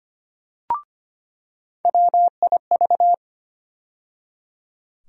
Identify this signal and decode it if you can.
Morse code, heard as a keyed tone, about 25 wpm, 700 Hz. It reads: WIV
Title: USA Call Sign Postfixes - 3 Letters 25wpm